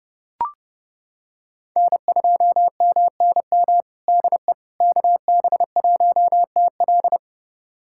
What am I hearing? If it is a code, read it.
Morse code, heard as a keyed tone, about 32 wpm, 700 Hz. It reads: N2MNM DE KB1TL
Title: USA Callsign DE Callsign - 5 Characters - 30wpm